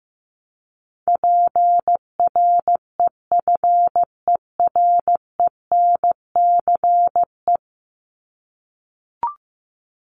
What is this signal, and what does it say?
Morse code, heard as a keyed tone, about 15 wpm, 700 Hz. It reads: PREFERENCE
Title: Top 1251-1500 Words 15wpm